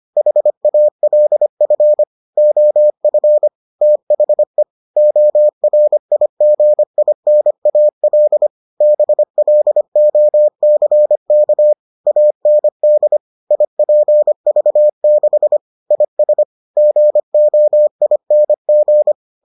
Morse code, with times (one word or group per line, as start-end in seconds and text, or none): 0.2-2.0 HALF
2.4-3.5 OF
3.8-4.6 THE
5.0-8.5 ORIGINAL
8.8-11.7 BLOCK
12.1-13.2 AND
13.5-15.6 IPV6
15.9-16.4 IS
16.8-19.1 GOING